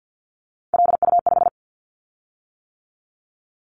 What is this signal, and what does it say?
Morse code, heard as a keyed tone, about 50 wpm, 690 Hz. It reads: LU5
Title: Callsign Prefixes for Top 100 DXCC - 50wpm